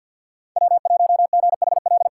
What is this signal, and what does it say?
Morse code, heard as a keyed tone, about 52 wpm, 700 Hz. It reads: W1GHL
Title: USA Callsign DE Callsign - 5 Characters - 50wpm